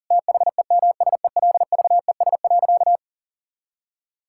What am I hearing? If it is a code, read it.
Morse code, heard as a keyed tone, about 37 wpm, 700 Hz. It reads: THEMSELVES.
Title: Sentences from Top 1000 Words - Review Part 5 - 40wpm